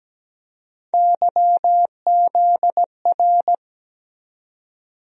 Morse code, heard as a keyed tone, about 17 wpm, 700 Hz. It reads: YZR
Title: USA Call Sign Postfixes - 3 Letters 17wpm